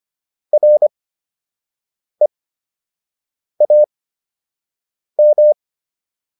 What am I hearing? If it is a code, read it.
Morse code, heard as a keyed tone, about 25 wpm, spaced as for 6 wpm, 600 Hz. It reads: REAM